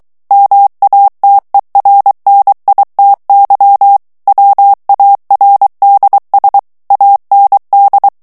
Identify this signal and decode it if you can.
Morse code, heard as a keyed tone, about 23 wpm, 800 Hz. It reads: MATERNITY WARDS AND
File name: morse mp3 / ANGLAIS texte clair 2 vitesse 22 mots minute